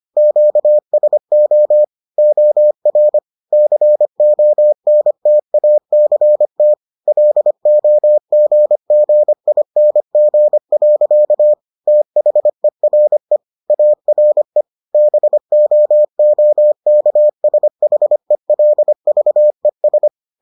Morse code, heard as a keyed tone, about 25 wpm, 600 Hz. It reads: QSO OR CONTACT LOGGING. THERE ARE BOOKSHELVES